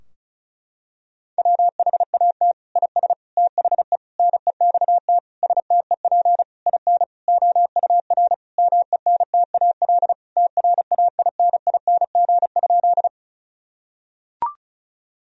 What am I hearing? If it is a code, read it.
Morse code, heard as a keyed tone, about 34 wpm, 700 Hz. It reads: WHAT IS THE NEXT STEP IN OUR MENTAL TRAINING?
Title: Sentences from Top 1000 Words - Review Part 4 - 35wpm